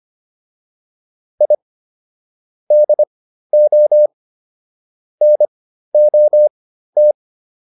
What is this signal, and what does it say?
Morse code, heard as a keyed tone, about 25 wpm, spaced as for 13 wpm, 600 Hz. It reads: I DO NOT